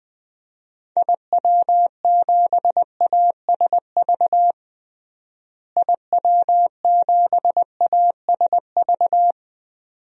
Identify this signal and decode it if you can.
Morse code, heard as a keyed tone, about 21 wpm, 700 Hz. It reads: IW7ASV IW7ASV
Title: DXCC-100 Call Signs - 6 Characters - Repeated 2x - 20wpm